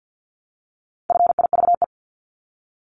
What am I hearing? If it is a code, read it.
Morse code, heard as a keyed tone, about 50 wpm, 700 Hz. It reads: FIVE
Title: Top 500 Words Repeated 50wpm